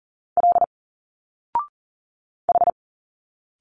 Morse code, heard as a keyed tone, about 40 wpm, 700 Hz. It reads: L H
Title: Single Letters 40wpm